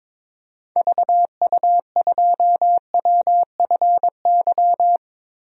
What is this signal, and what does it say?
Morse code, heard as a keyed tone, about 22 wpm, 700 Hz. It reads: VU2WFY
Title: DXCC-100 Call Signs - 6 Characters - 22wpm